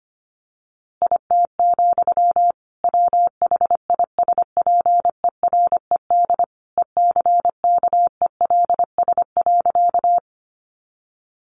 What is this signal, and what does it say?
Morse code, heard as a keyed tone, about 25 wpm, 700 Hz. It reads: IT, WHISPERED ECKELS.
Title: A Sound Of Thunder 25wpm